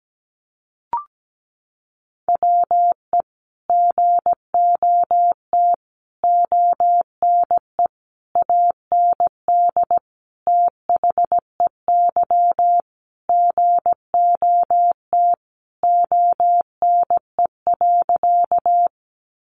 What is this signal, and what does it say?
Morse code, heard as a keyed tone, about 17 wpm, 700 Hz. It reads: WE GOT ONE AND THEY GOT ONE.
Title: Sentences from Top 300 Words 17wpm